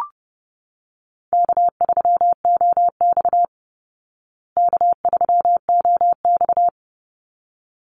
Morse code, heard as a keyed tone, about 30 wpm, 700 Hz. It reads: K3OX K3OX
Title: Easy Call Signs Repeated 2x - 30wpm